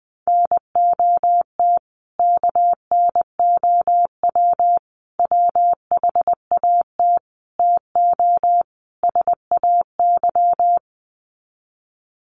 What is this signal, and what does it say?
Morse code, heard as a keyed tone, about 20 wpm, 700 Hz. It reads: NOT KNOW WHAT TO SAY
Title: Sets of 5 Words 20wpm